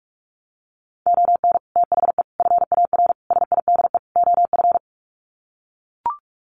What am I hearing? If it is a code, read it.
Morse code, heard as a keyed tone, about 44 wpm, 700 Hz. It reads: ON THE FAR SIDE OF
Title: Sets of 5 Words 45wpm